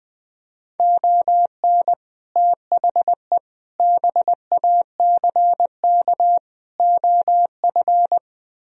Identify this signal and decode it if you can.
Morse code, heard as a keyed tone, about 20 wpm, 700 Hz. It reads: ON THE BACK OF
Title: Speed-Racing - Sets of 4 Words 17wpm